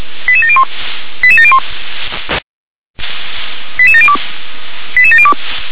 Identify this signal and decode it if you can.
Five-tone selective call (ZVEI), heard as quick runs of five tones, tones of about 70 ms. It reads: 88701 88701 88702 88702